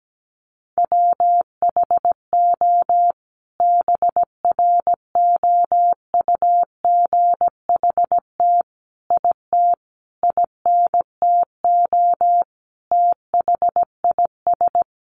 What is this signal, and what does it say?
Morse code, heard as a keyed tone, about 17 wpm, 700 Hz. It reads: WHO BROUGHT IT INTO THIS
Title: Sentences from Top 600 Words 17wpm